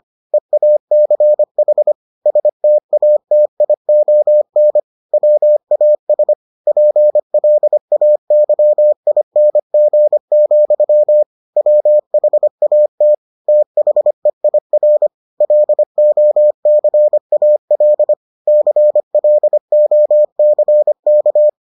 Morse code, heard as a keyed tone, about 25 wpm, 600 Hz. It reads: EACH STATION WAS PLAYING, WHAT THEIR LOCAL CLOCK